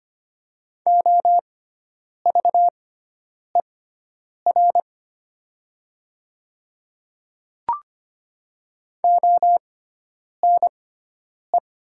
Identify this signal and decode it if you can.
Morse code, heard as a keyed tone, about 25 wpm, spaced as for 9 wpm, 700 Hz. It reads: OVER ONE
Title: 6x Character Spacing - Top 200 Words - 25wpm